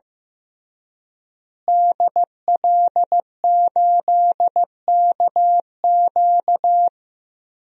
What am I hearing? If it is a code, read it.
Morse code, heard as a keyed tone, about 16 wpm, 700 Hz. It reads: DL8KQ
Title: DXCC-100 Call Signs - 5 Characters - Repeated 3x - 15wpm